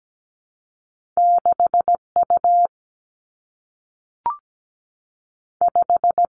Morse code, heard as a keyed tone, about 17 wpm, 700 Hz. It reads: 6U 5
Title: Two Challenging Letter-Number Combinations 17wpm